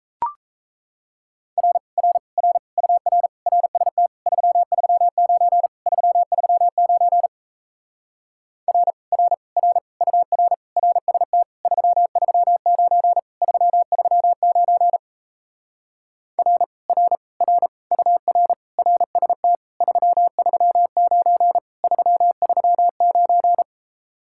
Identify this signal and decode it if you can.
Morse code, heard as a keyed tone, about 37 wpm, 700 Hz. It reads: R R R UR RST 339 339 R R R UR RST 339 339 R R R UR RST 339 339